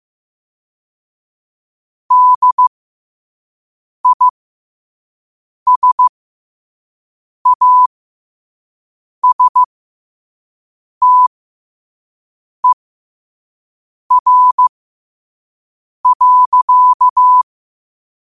Morse code, heard as a keyed tone, about 15 wpm, spaced as for 5 wpm, 1000 Hz. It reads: DISASTER.
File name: qotdfiles / 5wpm